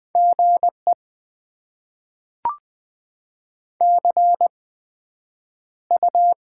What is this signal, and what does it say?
Morse code, heard as a keyed tone, about 20 wpm, spaced as for 9 wpm, 700 Hz. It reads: GE CU
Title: Top 20 QSO Elements - 8x Character Spacing - 20wpm